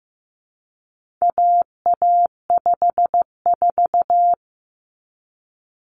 Morse code, heard as a keyed tone, about 15 wpm, 700 Hz. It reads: AA54
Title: Common Cut Numbers - 4-Digit 15wpm